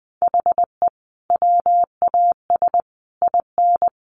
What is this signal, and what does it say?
Morse code, heard as a keyed tone, about 19 wpm, 700 Hz. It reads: HE WAS IN
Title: Sentences from Top 1300 Words 20wpm